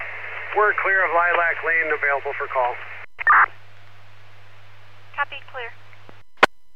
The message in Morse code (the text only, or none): none